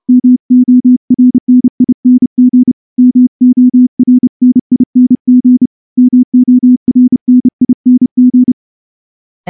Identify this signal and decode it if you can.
Morse code, heard as a keyed tone, about 29 wpm, 250 Hz. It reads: MORNING MORNING MORNING
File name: morse-training-30WPM